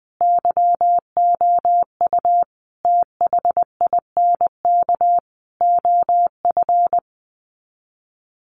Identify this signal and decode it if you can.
Morse code, heard as a keyed tone, about 20 wpm, 700 Hz. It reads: YOU THINK OF
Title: Sets of 5 Words 20wpm